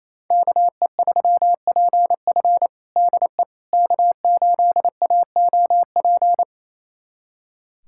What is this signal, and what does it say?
Morse code, heard as a keyed tone, about 28 wpm, 700 Hz. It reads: KE3PF DE K8AOP
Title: USA Callsign DE Callsign - 5 Characters - 28wpm